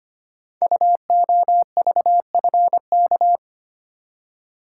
Morse code, heard as a keyed tone, about 25 wpm, 700 Hz. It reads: UOVFK
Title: Code Groups - 5 Letters - 25wpm